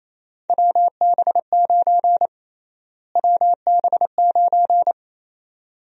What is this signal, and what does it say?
Morse code, heard as a keyed tone, about 28 wpm, 700 Hz. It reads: WB9 WB9
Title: USA Call Sign Prefixes - Repeated 2x - 28wpm